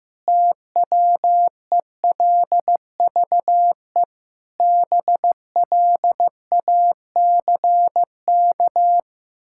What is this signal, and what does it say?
Morse code, heard as a keyed tone, about 15 wpm, 700 Hz. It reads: TWELVE BLACK
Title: Pangram Challenge v2 - 15wpm